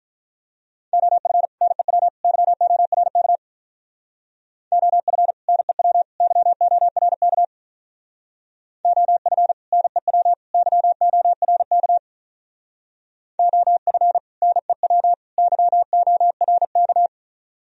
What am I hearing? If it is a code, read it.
Morse code, heard as a keyed tone, about 44 wpm, 690 Hz. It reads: OF NEW YORK OF NEW YORK OF NEW YORK OF NEW YORK